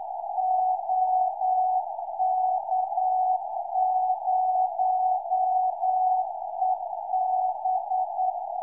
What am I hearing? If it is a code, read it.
Morse code, heard as a keyed tone, about 9 wpm, 740 Hz, with radio noise. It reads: OK0EB